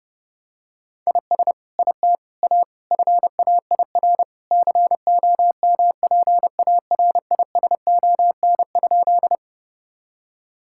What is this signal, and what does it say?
Morse code, heard as a keyed tone, about 30 wpm, 700 Hz. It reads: IS IT A FAIR COMPARISON?
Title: Sentences from Top 1600 Words 30wpm